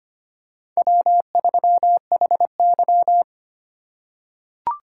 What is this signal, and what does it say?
Morse code, heard as a keyed tone, about 25 wpm, 700 Hz. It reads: W3HY
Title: USA Call Signs - 4 Characters - 4x Character Spacing - 25wpm